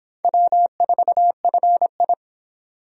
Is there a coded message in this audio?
Yes: Morse code from 0.2 to 2.1 s, heard as a keyed tone, about 26 wpm, 700 Hz. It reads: W4FI